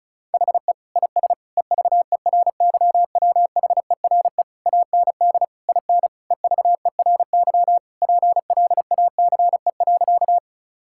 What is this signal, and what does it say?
Morse code, heard as a keyed tone, about 36 wpm, 700 Hz. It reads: HE IS EVERYWHERE AND IN EVERY PLACE.
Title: Sentences from Top 1600 Words 35wpm